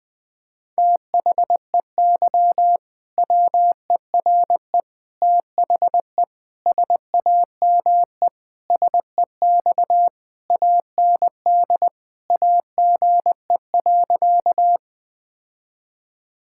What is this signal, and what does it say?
Morse code, heard as a keyed tone, about 21 wpm, 700 Hz. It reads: THEY WERE THE SAME SEX AND AGE.
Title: Sentences from Top 1600 Words 20wpm